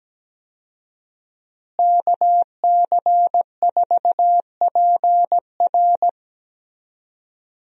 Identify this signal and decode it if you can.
Morse code, heard as a keyed tone, about 17 wpm, 700 Hz. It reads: KC4PR
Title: USA Call Signs - 5 Characters - Repeated 2x - 17wpm